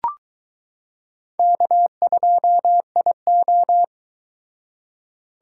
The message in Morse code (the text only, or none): K2IO